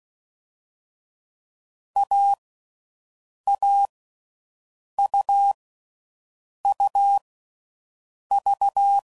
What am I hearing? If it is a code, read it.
Morse code, heard as a keyed tone, about 16 wpm, 800 Hz. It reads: A A U U V
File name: Combinations / Difficult Combos / AUV AUV4